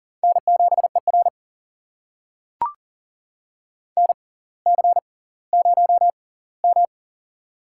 Morse code, heard as a keyed tone, about 39 wpm, spaced as for 20 wpm, 700 Hz. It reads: N7ER NC0M
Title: USA Call Signs - 4 Characters - 6x Character Spacing - 40wpm